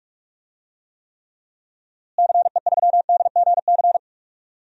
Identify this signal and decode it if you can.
Morse code, heard as a keyed tone, about 43 wpm, 700 Hz. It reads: CE3DGC
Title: DXCC-100 Call Signs - 6 Characters - Repeated 2x - 45wpm